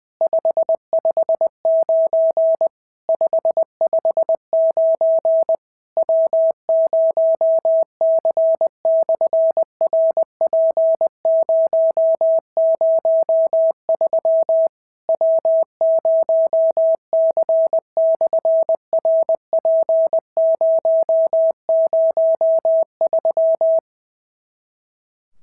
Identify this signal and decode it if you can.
Morse code, heard as a keyed tone, about 20 wpm, 650 Hz. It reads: 559 559 W0C/RP003 W0C/RP003
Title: SOTA QSO Practice 20wpm